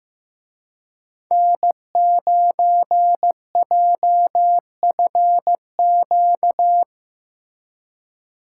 Morse code, heard as a keyed tone, about 17 wpm, 700 Hz. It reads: N9JFQ
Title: Speed Racing - USA 1x3 Call Signs 15wpm